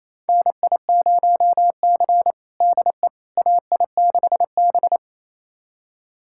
Morse code, heard as a keyed tone, about 28 wpm, 700 Hz. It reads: NI0C DE AI6B
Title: USA Callsign DE Callsign - 4 Characters - 28wpm